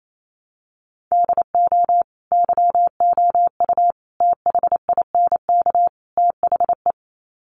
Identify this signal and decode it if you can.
Morse code, heard as a keyed tone, about 28 wpm, 700 Hz. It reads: DO YOU THINK THE
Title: Sets of 4 Words 28wpm